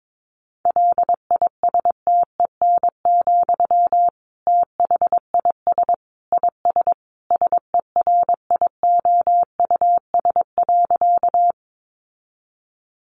Morse code, heard as a keyed tone, about 22 wpm, 700 Hz. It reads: LISTEN, THIS IS SERIOUS.
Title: Sentences from Top 1100 Words 22wpm